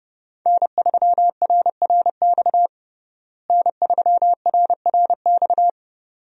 Morse code, heard as a keyed tone, about 30 wpm, 700 Hz. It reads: N3RRX N3RRX
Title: USA Call Signs - 5 Characters - Repeated 2x - 30wpm